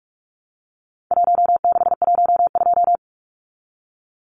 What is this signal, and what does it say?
Morse code, heard as a keyed tone, about 44 wpm, 700 Hz. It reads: 1612
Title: Numbers - Contest Style - 4-Digit 45wpm